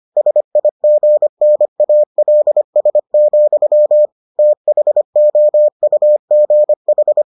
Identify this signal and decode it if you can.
Morse code, heard as a keyed tone, about 25 wpm, 600 Hz. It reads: SIGNALS, THOUGH